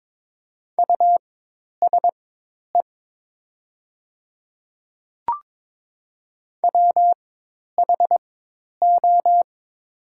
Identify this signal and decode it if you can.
Morse code, heard as a keyed tone, about 22 wpm, spaced as for 10 wpm, 700 Hz. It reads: USE WHO
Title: 4x Character Spacing - Top 100 Words - 22wpm